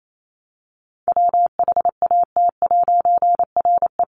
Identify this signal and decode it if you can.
Morse code, heard as a keyed tone, about 28 wpm, 700 Hz. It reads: WHAT'RE
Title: A Sound Of Thunder 28wpm